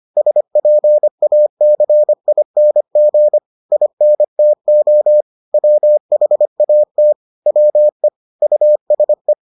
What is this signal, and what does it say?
Morse code, heard as a keyed tone, about 25 wpm, 600 Hz. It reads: SPACING INTO WHAT WE USE